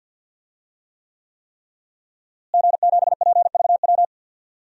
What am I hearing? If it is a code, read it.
Morse code, heard as a keyed tone, about 49 wpm, 700 Hz. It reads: G7PVW
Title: DXCC-100 Call Signs - 5 Characters - Repeated 3x - 50wpm